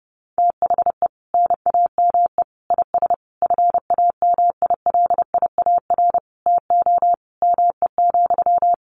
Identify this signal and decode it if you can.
Morse code, heard as a keyed tone, about 30 wpm, 700 Hz. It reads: THE NAME IS FAMILIAR TO ME,
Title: Sentences from Top 1000 Words 30wpm